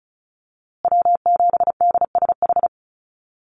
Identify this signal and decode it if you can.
Morse code, heard as a keyed tone, about 34 wpm, 700 Hz. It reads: W7DSH
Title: USA Call Signs Repeated 3x 35wpm